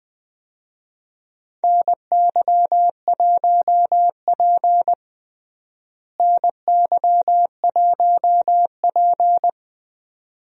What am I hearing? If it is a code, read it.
Morse code, heard as a keyed tone, about 20 wpm, 700 Hz. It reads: NY1P NY1P
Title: USA Call Signs - 4 Characters - Repeated 2x - 20wpm